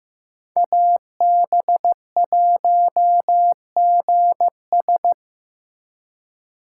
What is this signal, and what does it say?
Morse code, heard as a keyed tone, about 15 wpm, 700 Hz. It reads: AB1GS
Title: USA Call Signs - 5 Characters - 8x Character Spacing - 15wpm